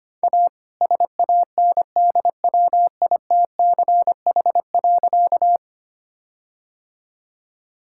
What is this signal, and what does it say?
Morse code, heard as a keyed tone, about 25 wpm, 700 Hz. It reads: A SANDWITCH.
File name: practice / 25wpm / All-Age Appropriate Jokes - 25wpm